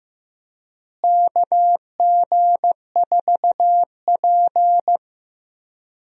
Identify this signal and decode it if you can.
Morse code, heard as a keyed tone, about 15 wpm, 700 Hz. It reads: KG4P
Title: USA Call Signs - 4 Characters - Repeated 3x - 15wpm